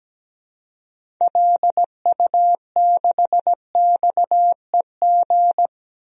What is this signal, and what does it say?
Morse code, heard as a keyed tone, about 17 wpm, 700 Hz. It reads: LU6XEG
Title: DXCC-100 Call Signs - 6 Characters - 17wpm